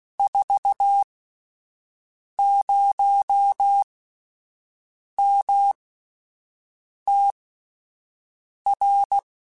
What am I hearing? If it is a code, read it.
Morse code, heard as a keyed tone, about 16 wpm, spaced as for 5 wpm, 800 Hz. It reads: 40MTR
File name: QSOs Spaced